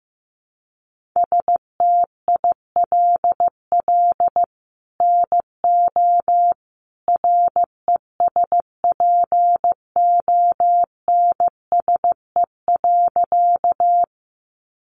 Morse code, heard as a keyed tone, about 15 wpm, 700 Hz. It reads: STILL NO RESPONSE.
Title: Sentences from Top 1400 Words 15wpm